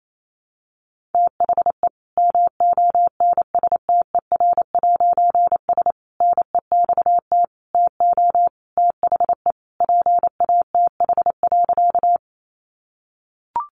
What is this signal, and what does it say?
Morse code, heard as a keyed tone, about 28 wpm, 700 Hz. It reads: THE MONSTER'S NEXT TO THE PATH.
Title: A Sound Of Thunder 28wpm